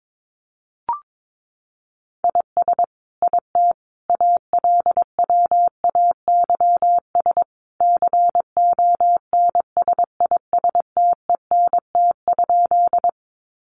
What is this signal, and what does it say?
Morse code, heard as a keyed tone, about 22 wpm, 700 Hz. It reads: IS IT ALWAYS CONSISTENT?
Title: Sentences from Top 1500 Words 22wpm